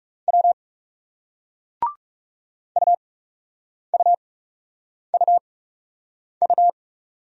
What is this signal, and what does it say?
Morse code, heard as a keyed tone, about 40 wpm, spaced as for 8 wpm, 700 Hz. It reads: W UUUU